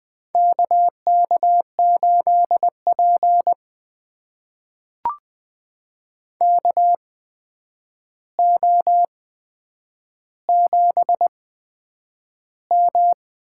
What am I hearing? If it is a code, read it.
Morse code, heard as a keyed tone, about 20 wpm, spaced as for 9 wpm, 700 Hz. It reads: KK8P KO7M
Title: USA Call Signs - 4 Characters - 8x Character Spacing - 20wpm